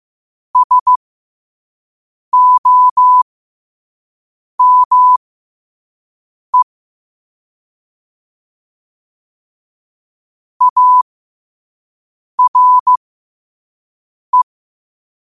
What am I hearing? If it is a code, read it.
Morse code, heard as a keyed tone, about 15 wpm, spaced as for 5 wpm, 1000 Hz. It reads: SOME ARE
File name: qotdfiles / 5wpm